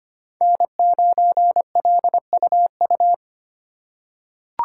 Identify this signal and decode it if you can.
Morse code, heard as a keyed tone, about 25 wpm, 700 Hz. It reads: N9LUU
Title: USA Call Signs - 5 Characters - 2x Character Spacing - 25wpm